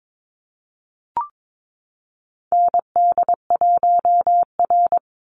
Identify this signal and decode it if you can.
Morse code, heard as a keyed tone, about 22 wpm, 700 Hz. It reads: ND1R
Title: Call Signs Repeated 2x - 22wpm